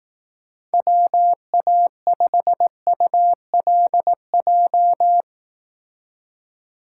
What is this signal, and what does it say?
Morse code, heard as a keyed tone, about 18 wpm, 700 Hz. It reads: WA5ULJ
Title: Speed Racing - USA 2x3 Call Signs 15wpm